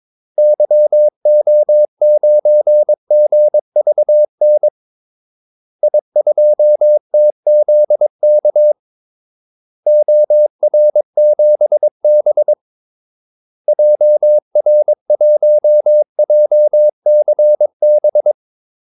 Morse code, heard as a keyed tone, about 22 wpm, 600 Hz. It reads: YO9GVN I2TZK OR7B JR1JCB
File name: Callsigns-22wpm